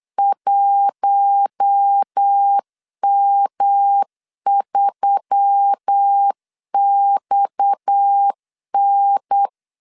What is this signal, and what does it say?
Morse code, heard as a keyed tone, about 8 wpm, 800 Hz. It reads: 1M3XN
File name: LEZIONI / 8 YJ45 / 8 YJ 8wpm 3313